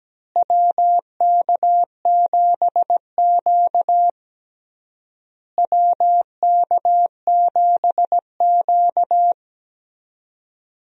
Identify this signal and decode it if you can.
Morse code, heard as a keyed tone, about 17 wpm, 700 Hz. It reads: WK7Q WK7Q
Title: USA Call Signs - 4 Characters - Repeated 3x - 17wpm